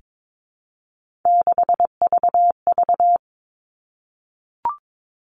Morse code, heard as a keyed tone, about 22 wpm, 700 Hz. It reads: 6VV
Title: All Cut Numbers - 3-Digit 22wpm